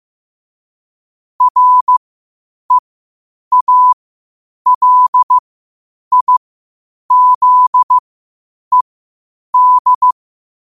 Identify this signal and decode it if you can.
Morse code, heard as a keyed tone, about 15 wpm, spaced as for 8 wpm, 1000 Hz. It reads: REALIZED
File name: qotdfiles / 8wpm